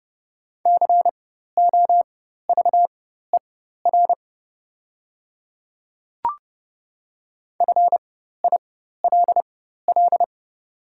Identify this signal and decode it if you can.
Morse code, heard as a keyed tone, about 30 wpm, spaced as for 14 wpm, 700 Hz. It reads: COVER FILL
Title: 4x Character Spacing - Top 500 Words - 30wpm